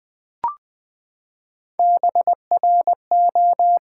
Morse code, heard as a keyed tone, about 20 wpm, 700 Hz. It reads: BRO
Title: Build A Word - 3 Letters At A Time - 9-Letter Words - 20wpm